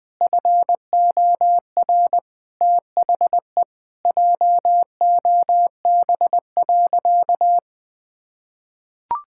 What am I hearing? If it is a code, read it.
Morse code, heard as a keyed tone, about 21 wpm, 700 Hz. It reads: FOR THE JOB.
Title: Sentences from Top 1600 Words 20wpm